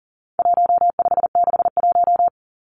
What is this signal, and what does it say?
Morse code, heard as a keyed tone, about 40 wpm, 700 Hz. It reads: 1561
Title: Numbers - 4-Digit 40wpm